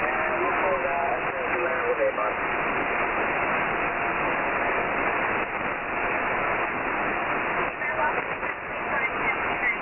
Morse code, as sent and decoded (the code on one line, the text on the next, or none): none